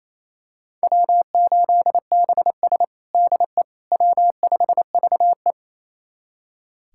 Morse code, heard as a keyed tone, about 28 wpm, 700 Hz. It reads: W8BS DE W5VE
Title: USA Callsign DE Callsign - 4 Characters - 28wpm